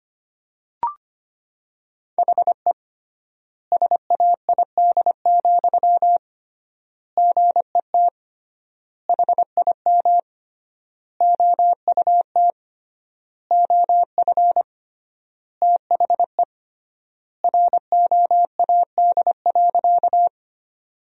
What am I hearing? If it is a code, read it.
Morse code, heard as a keyed tone, about 25 wpm, 700 Hz. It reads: HE SAID, GET HIM OUT OF THE ROAD.
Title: 3x Word-Spacing - Sentences from Top 400 Words - 25wpm